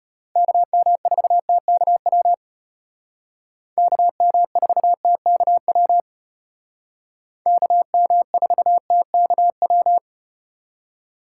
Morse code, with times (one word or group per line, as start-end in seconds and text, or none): 0.4-2.3 KM4TKW
3.8-6.0 KM4TKW
7.5-10.0 KM4TKW